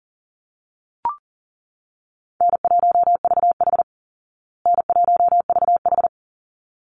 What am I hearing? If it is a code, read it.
Morse code, heard as a keyed tone, about 37 wpm, 700 Hz. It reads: N1VH N1VH
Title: Call Signs Repeated 3x - 40wpm